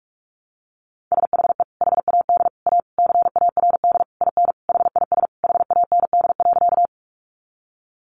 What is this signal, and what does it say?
Morse code, heard as a keyed tone, about 44 wpm, 700 Hz. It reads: SHE HAD A CARD IN HIS HAND.
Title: Sentences from Top 600 Words 45wpm